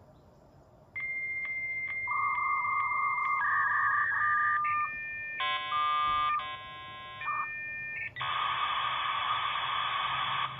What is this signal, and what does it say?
Dial-up modem handshake.
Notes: Connecting at 33600 bps with LAPM/V.42/V42.BIS, G7.11 u-law codec